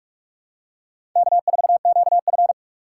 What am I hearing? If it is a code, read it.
Morse code, heard as a keyed tone, about 46 wpm, 700 Hz. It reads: K4QF
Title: USA Call Signs - 4 Characters - 4x Character Spacing - 45wpm